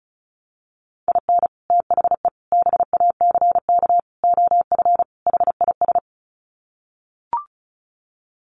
Morse code, heard as a keyed tone, about 34 wpm, 700 Hz. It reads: IN THE BACK OF HIS
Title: Sets of 5 Words Encore2 35wpm